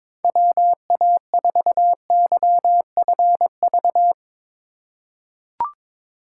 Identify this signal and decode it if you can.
Morse code, heard as a keyed tone, about 22 wpm, 700 Hz. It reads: WA4YFV
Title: USA Call Signs - 6 Characters - 4x Character Spacing - 22wpm